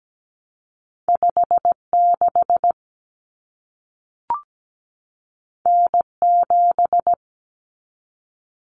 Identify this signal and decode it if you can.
Morse code, heard as a keyed tone, about 17 wpm, 700 Hz. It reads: 56 N7
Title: Common Cut Numbers - 2-Digit 17wpm